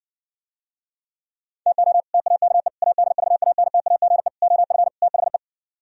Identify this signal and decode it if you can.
Morse code, heard as a keyed tone, about 59 wpm, 700 Hz. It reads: TO TAKE ADVANTAGE OF THE